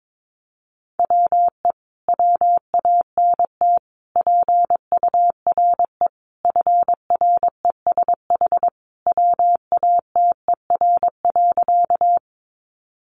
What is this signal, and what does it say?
Morse code, heard as a keyed tone, about 22 wpm, 700 Hz. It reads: WE WANT PURE FRESH WATER.
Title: Sentences from Top 1500 Words 22wpm